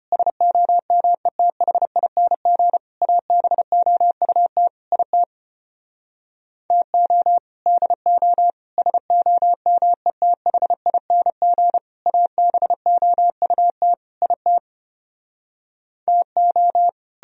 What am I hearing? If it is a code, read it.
Morse code, heard as a keyed tone, about 33 wpm, 700 Hz. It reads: SOMETHING ABOUT IT TO DO SOMETHING ABOUT IT TO